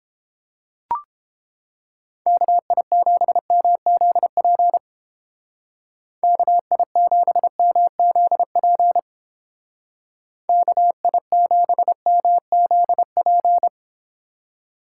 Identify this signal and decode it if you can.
Morse code, heard as a keyed tone, about 29 wpm, 700 Hz. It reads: KI7MZP KI7MZP KI7MZP